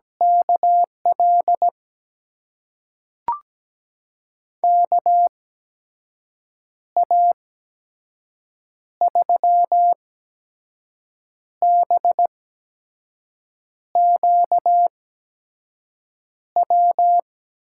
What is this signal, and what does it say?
Morse code, heard as a keyed tone, about 17 wpm, spaced as for 5 wpm, 700 Hz. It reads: KL KA3BQW
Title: USA Call Signs - 6 Characters - 8x Character Spacing - 17wpm